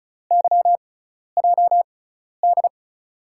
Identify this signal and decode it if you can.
Morse code, heard as a keyed tone, about 36 wpm, spaced as for 12 wpm, 700 Hz. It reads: YJD
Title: USA Call Signs - 6 Characters - 6x Character Spacing - 35wpm